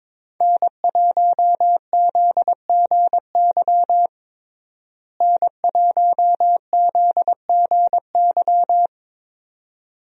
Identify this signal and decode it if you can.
Morse code, heard as a keyed tone, about 22 wpm, 700 Hz. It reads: N1ZGY N1ZGY
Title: USA Call Signs - 5 Characters - Repeated 2x - 22wpm